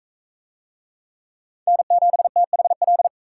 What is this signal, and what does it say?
Morse code, heard as a keyed tone, about 41 wpm, 700 Hz. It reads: N7THL